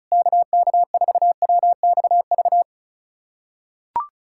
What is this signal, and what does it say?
Morse code, heard as a keyed tone, about 36 wpm, 700 Hz. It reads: KK4WXV
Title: USA Callsign DE Callsign - 6 Characters - 35wpm